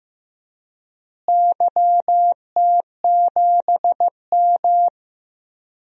Morse code, heard as a keyed tone, about 15 wpm, 700 Hz. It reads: YT7M
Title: DXCC-100 Call Signs - 4 Characters - Repeated 3x - 15wpm